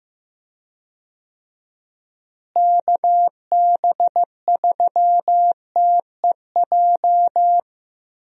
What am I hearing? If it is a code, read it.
Morse code, heard as a keyed tone, about 15 wpm, 700 Hz. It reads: KB3TEJ
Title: USA Call Signs - 6 Characters - Repeated 2x - 15wpm